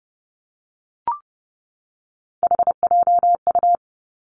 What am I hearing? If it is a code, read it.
Morse code, heard as a keyed tone, about 30 wpm, 700 Hz. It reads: HJU
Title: Three Challenging Letter-Number Combinations 30wpm